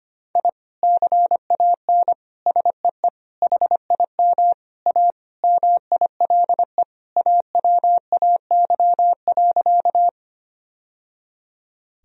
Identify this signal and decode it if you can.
Morse code, heard as a keyed tone, about 25 wpm, 700 Hz. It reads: I CAN SEE HIM A MILE AWAY.
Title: Sentences from Top 1000 Words - Review Part 1 - 25wpm